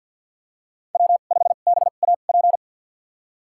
Morse code, heard as a keyed tone, about 48 wpm, spaced as for 35 wpm, 700 Hz. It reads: W5BAP
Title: USA Call Signs - 5 Characters - 2x Character Spacing - 50wpm